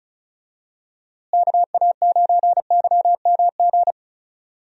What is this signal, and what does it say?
Morse code, heard as a keyed tone, about 34 wpm, 700 Hz. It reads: KA9YMG
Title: USA Call Signs - 6 Characters - Repeated 3x - 35wpm